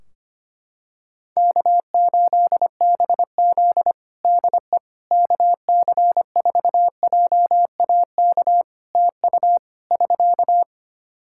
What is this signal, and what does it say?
Morse code, heard as a keyed tone, about 25 wpm, 700 Hz. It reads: K8BZ DE KC4JAK TU <SK>
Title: Practice QSO - Exchange 4 - Ending - 25wpm